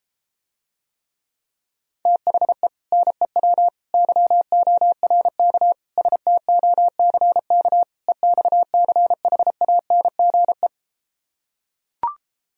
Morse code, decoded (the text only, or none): THE NEW YORK STOCK EXCHANGE